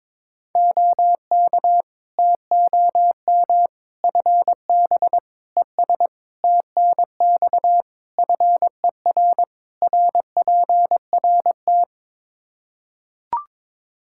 Morse code, heard as a keyed tone, about 22 wpm, 700 Hz. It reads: OK TOM FB ES TNX FER RPRT
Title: Practice QSO - Exchange 2 - Info - 22wpm